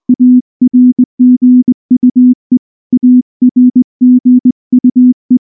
Morse code, heard as a keyed tone, about 18 wpm, 250 Hz. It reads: ARGUE ARGUE